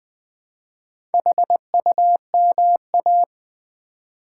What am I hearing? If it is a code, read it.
Morse code, heard as a keyed tone, about 20 wpm, 700 Hz. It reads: HUMA
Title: Build A Word - 4 Letters At A Time - 12-Letter Words - 20wpm